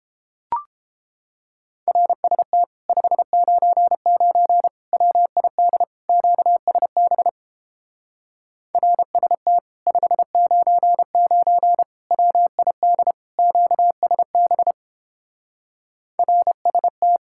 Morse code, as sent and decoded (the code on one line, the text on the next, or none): .-. ... - / ..... ----. ----. / .-- .. -.. / --.- ... -... / .-. ... - / ..... ----. ----. / .-- .. -.. / --.- ... -... / .-. ... -
RST 599 WID QSB RST 599 WID QSB RST